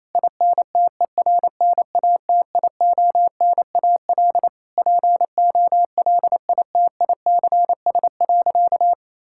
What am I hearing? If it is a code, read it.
Morse code, heard as a keyed tone, about 28 wpm, 700 Hz. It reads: INTERNATIONAL POLITICS.
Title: Sentences from Top 1000 Words - Review Part 6 - 28wpm